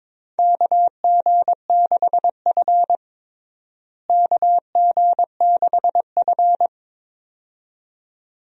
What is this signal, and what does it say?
Morse code, heard as a keyed tone, about 22 wpm, 700 Hz. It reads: KG6F KG6F
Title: USA Call Signs - 4 Characters - Repeated 2x - 22wpm